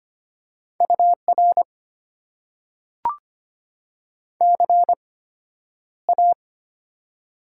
Morse code, heard as a keyed tone, about 25 wpm, spaced as for 11 wpm, 690 Hz. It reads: UR CA
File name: Top 90 QSO Elements - 8x Character Spacing - 25wpm